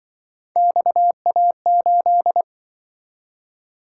Morse code, heard as a keyed tone, about 24 wpm, 700 Hz. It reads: XA8
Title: Speed-Racing - Three Letter-Number Combinations - 20wpm